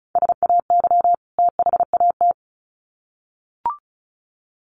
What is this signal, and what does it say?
Morse code, heard as a keyed tone, about 34 wpm, 700 Hz. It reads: SAY THAT
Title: Sets of 2 Words 35wpm